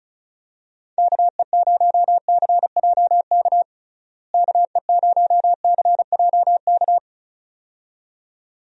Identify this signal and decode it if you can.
Morse code, heard as a keyed tone, about 36 wpm, 700 Hz. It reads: KE0CJK KE0CJK
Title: USA Call Signs - 6 Characters - Repeated 2x - 35wpm